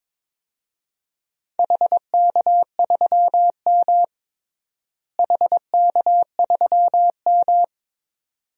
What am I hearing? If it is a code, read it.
Morse code, heard as a keyed tone, about 22 wpm, 700 Hz. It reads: HK3M HK3M
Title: DXCC-100 Call Signs - 4 Characters - Repeated 2x - 22wpm